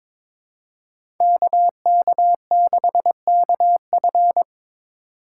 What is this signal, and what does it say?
Morse code, heard as a keyed tone, about 23 wpm, 700 Hz. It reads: KK6KF
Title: USA Call Signs - 5 Characters - 3x Character Spacing - 22wpm